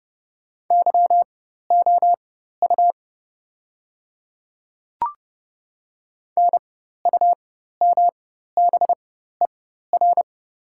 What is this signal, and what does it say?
Morse code, heard as a keyed tone, about 30 wpm, spaced as for 14 wpm, 700 Hz. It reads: YOU NUMBER
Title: 4x Character Spacing - Top 100 Words - 30wpm